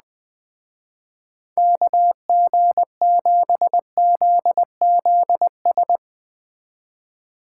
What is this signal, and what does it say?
Morse code, heard as a keyed tone, about 20 wpm, 700 Hz. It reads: KG7ZZS
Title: Speed Racing - USA 2x3 Call Signs 17wpm